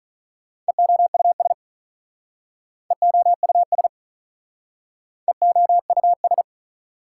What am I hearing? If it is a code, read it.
Morse code, heard as a keyed tone, about 43 wpm, 700 Hz. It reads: EOUS EOUS EOUS
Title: Speed-Racing Prefixes and Suffixes 35wpm